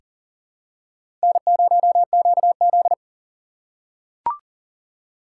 Morse code, heard as a keyed tone, about 40 wpm, 690 Hz. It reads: N0QZ
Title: USA Call Signs - 4 Characters - 2x Character Spacing - 40wpm